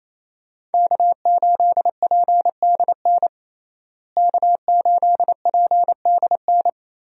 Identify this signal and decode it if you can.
Morse code, heard as a keyed tone, about 28 wpm, 700 Hz. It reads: K8PDN K8PDN
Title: USA Call Signs - 5 Characters - Repeated 2x - 28wpm